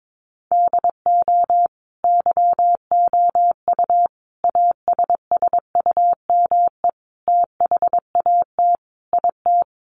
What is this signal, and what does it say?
Morse code, heard as a keyed tone, about 22 wpm, 700 Hz. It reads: DO YOU ASSUME THAT IT
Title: Sentences from Top 1500 Words 22wpm